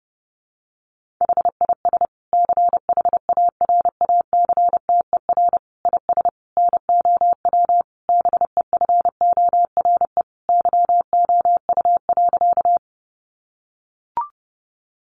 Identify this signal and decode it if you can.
Morse code, heard as a keyed tone, about 30 wpm, 700 Hz. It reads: HIS CHARACTER IS NOW BEFORE YOU.
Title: Sentences from Top 1200 Words 30wpm